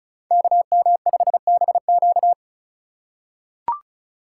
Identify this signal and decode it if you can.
Morse code, heard as a keyed tone, about 36 wpm, 700 Hz. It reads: KM5BQ
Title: USA Call Signs - 5 Characters - 6x Character Spacing - 35wpm